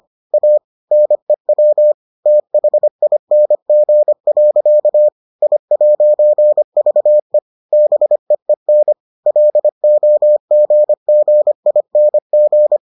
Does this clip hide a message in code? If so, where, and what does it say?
Yes: Morse code from 0.3 to 12.8 s, heard as a keyed tone, about 25 wpm, 600 Hz. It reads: A NEW THING. I'VE BEEN LOGGING